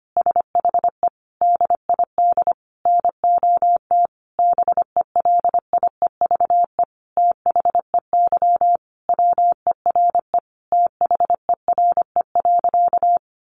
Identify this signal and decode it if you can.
Morse code, heard as a keyed tone, about 25 wpm, 700 Hz. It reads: SHE DID NOT BELIEVE THEY WERE THERE.
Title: Sentences from Top 1000 Words 25wpm